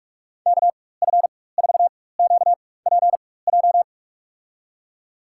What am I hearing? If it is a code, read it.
Morse code, heard as a keyed tone, about 45 wpm, spaced as for 21 wpm, 700 Hz. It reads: KF4QPJ
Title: USA Call Signs - 4x Character Spacing - 45wpm